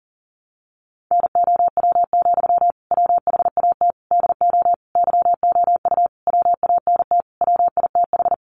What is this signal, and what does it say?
Morse code, heard as a keyed tone, about 40 wpm, 700 Hz. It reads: NOW, WHAT DO YOU WANT WITH
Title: Sentences from Top 200 Words 40wpm